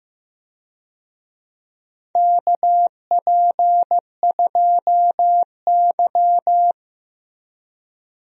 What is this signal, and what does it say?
Morse code, heard as a keyed tone, about 15 wpm, 700 Hz. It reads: KP2Y
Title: DXCC-100 Call Signs - 4 Characters - Repeated 2x - 15wpm